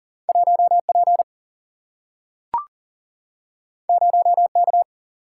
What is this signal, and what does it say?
Morse code, heard as a keyed tone, about 40 wpm, 700 Hz. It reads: 1 P 0 K
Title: Single-Letter Single-Number 40wpm